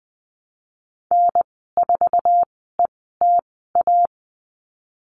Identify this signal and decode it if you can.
Morse code, heard as a keyed tone, about 20 wpm, spaced as for 15 wpm, 700 Hz. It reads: N4ETA
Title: 055 Lesson 10g - Letter r - Call Signs 20wpm